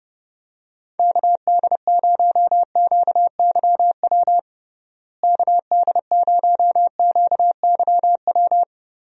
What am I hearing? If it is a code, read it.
Morse code, heard as a keyed tone, about 31 wpm, 700 Hz. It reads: KD0QYW KD0QYW
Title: USA Call Signs - 6 Characters - Repeated 2x - 30wpm